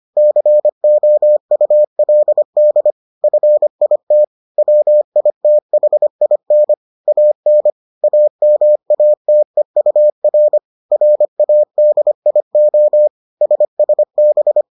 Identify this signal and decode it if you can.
Morse code, heard as a keyed tone, about 25 wpm, 600 Hz. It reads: COULD FIT WITHIN AN AMATEUR RADIO SSB